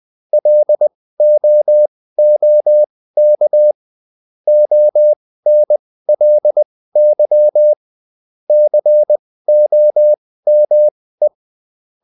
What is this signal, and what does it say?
Morse code, heard as a keyed tone, about 20 wpm, spaced as for 15 wpm, 600 Hz. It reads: LOOK ONLY COME
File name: Top-100-Words-15